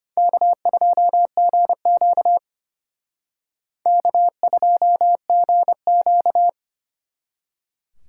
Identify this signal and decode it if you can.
Morse code, heard as a keyed tone, about 32 wpm, 700 Hz. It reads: K2GQ K2GQ